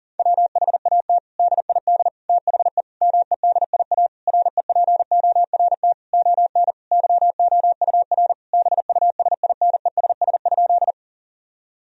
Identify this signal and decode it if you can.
Morse code, heard as a keyed tone, about 40 wpm, 700 Hz. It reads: WHAT DID THE MEDIA REPORT ON YOUR BUSINESS?
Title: Sentences from Top 1000 Words - Review Part 4 - 40wpm